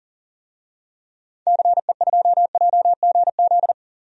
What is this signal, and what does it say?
Morse code, heard as a keyed tone, about 40 wpm, 700 Hz. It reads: CE2JGZ
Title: DXCC-100 Call Signs - 6 Characters - 40wpm